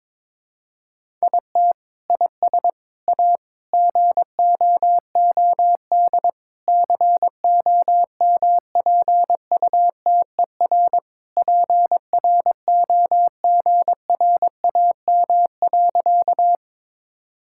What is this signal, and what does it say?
Morse code, heard as a keyed tone, about 22 wpm, 700 Hz. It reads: IT IS A GOOD COMPUTER PROGRAM.
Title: Sentences from Top 1000 Words - Review Part 5 - 22wpm